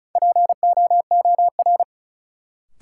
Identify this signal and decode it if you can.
Morse code, heard as a keyed tone, about 36 wpm, 700 Hz. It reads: POOR
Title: Sentences from Top 1000 Words - New Word Drill - 35wpm